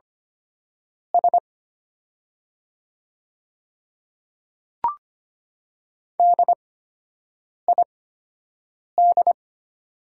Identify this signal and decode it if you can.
Morse code, heard as a keyed tone, about 25 wpm, spaced as for 7 wpm, 700 Hz. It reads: S DID
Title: 8x Character Spacing - Top 100 Words - 25wpm